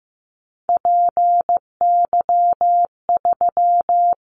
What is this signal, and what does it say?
Morse code, heard as a keyed tone, about 15 wpm, 700 Hz. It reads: PY3
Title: Callsign Prefixes for Top 50 DXCC - 15wpm